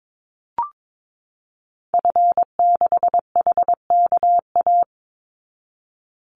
Morse code, heard as a keyed tone, about 22 wpm, 700 Hz. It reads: F6HKA
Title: Call Signs Repeated 2x - 22wpm